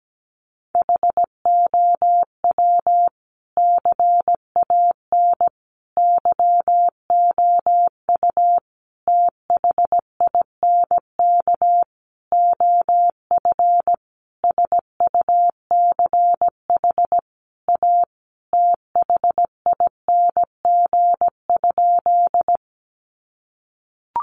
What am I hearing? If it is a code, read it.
Morse code, heard as a keyed tone, about 17 wpm, 700 Hz. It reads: HOW CAN YOU THINK OF SUCH A THING?
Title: Sentences from Top 200 Words 17wpm